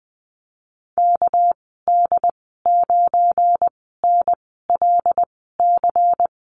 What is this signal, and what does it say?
Morse code, heard as a keyed tone, about 20 wpm, spaced as for 15 wpm, 700 Hz. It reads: KD9NLC
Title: 219 Lesson 35g - Number 0 - Call Signs 20wpm